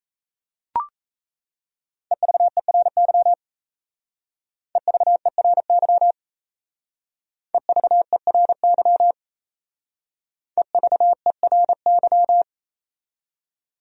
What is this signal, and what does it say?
Morse code, heard as a keyed tone, about 33 wpm, 700 Hz. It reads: EVERY EVERY EVERY EVERY
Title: Speed-Racing Top 500 Words 28wpm